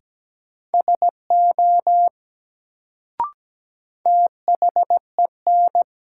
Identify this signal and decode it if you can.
Morse code, heard as a keyed tone, about 17 wpm, 700 Hz. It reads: SO THEN
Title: Sets of 2 Words - New Word Drill - 17wpm